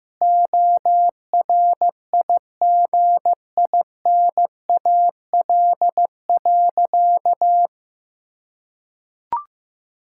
Morse code, heard as a keyed tone, about 15 wpm, 700 Hz. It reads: ORIGINAL.
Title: Sentences from Top 1000 Words - Review Part 6 - 15wpm